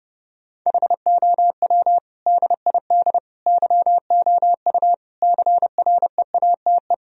Morse code, heard as a keyed tone, about 32 wpm, 700 Hz. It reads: HOW DID YOU CREATE
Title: Sentences from Top 1000 Words - Review Part 3 - 30wpm